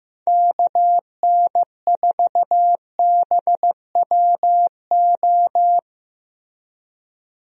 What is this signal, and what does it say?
Morse code, heard as a keyed tone, about 16 wpm, 700 Hz. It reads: KN4BWO
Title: USA Call Signs - 6 Characters - Repeated 2x - 15wpm